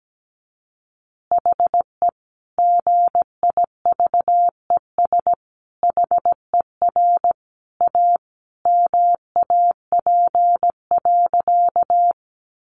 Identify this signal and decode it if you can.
Morse code, heard as a keyed tone, about 17 wpm, 700 Hz. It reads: HE GIVES HER A MAP.
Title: Additional Easy Phrases 17wpm